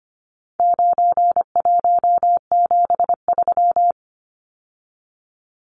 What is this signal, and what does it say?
Morse code, heard as a keyed tone, about 25 wpm, 700 Hz. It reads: 9173
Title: Numbers - 4-Digit 25wpm